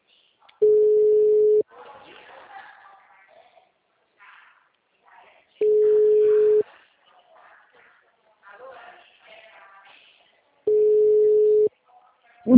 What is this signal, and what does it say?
Ringback tone (Europe, 425 Hz), with speech.